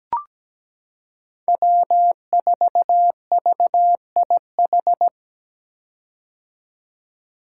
Morse code, heard as a keyed tone, about 17 wpm, 700 Hz. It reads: W4VIH
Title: Code Groups - 5 Letter:Numbers - 17wpm